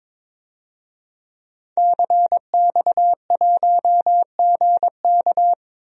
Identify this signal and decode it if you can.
Morse code, heard as a keyed tone, about 22 wpm, 700 Hz. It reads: CX1GK
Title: DXCC-100 Call Signs - 5 Characters - 22wpm